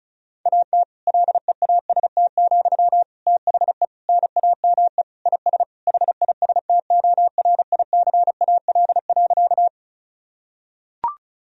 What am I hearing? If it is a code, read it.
Morse code, heard as a keyed tone, about 36 wpm, 700 Hz. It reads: AT LEAST, THE NAME IS HISTORICAL.
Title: Sentences from Top 1000 Words - Review Part 4 - 35wpm